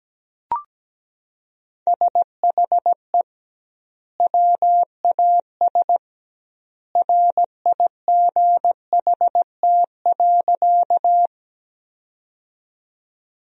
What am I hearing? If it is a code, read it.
Morse code, heard as a keyed tone, about 17 wpm, 700 Hz. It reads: SHE WAS RIGHT.
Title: 2x Word-Spacing - Sentences from Top 200 Words - 17wpm